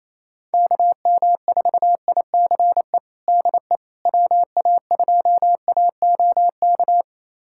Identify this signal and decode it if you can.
Morse code, heard as a keyed tone, about 27 wpm, 700 Hz. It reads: KM4ICE DE WA2AOK
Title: USA Callsign DE Callsign - 6 Characters - 28wpm